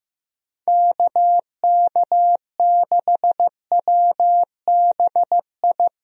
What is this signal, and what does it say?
Morse code, heard as a keyed tone, about 15 wpm, 700 Hz. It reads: KK6WBI
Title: USA Call Signs - 6 Characters - Repeated 3x - 15wpm